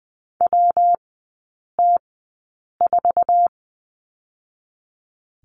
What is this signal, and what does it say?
Morse code, heard as a keyed tone, about 20 wpm, 700 Hz. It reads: W T 4
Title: 114 Lesson 19e - Letter w - Three Characters 20wpm